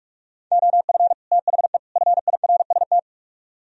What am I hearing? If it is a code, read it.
Morse code, heard as a keyed tone, about 57 wpm, 700 Hz. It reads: OF THE FIRST